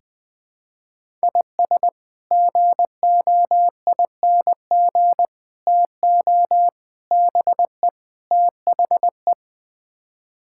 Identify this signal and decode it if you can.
Morse code, heard as a keyed tone, about 20 wpm, 700 Hz. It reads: IS GOING TO BE THE
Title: Build A Phrase - 5-Word Phrases - 20wpm